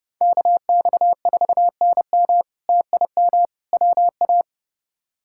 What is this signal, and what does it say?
Morse code, heard as a keyed tone, about 29 wpm, 700 Hz. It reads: KX4NM TIM WA
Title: K1USN SST QSO Practice - 30wpm